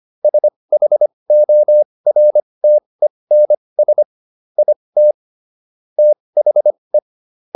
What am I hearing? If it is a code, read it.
Morse code, heard as a keyed tone, about 25 wpm, spaced as for 20 wpm, 600 Hz. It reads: SHORTENS IT THE
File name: SS316 20